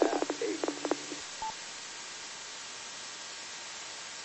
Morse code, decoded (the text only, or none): none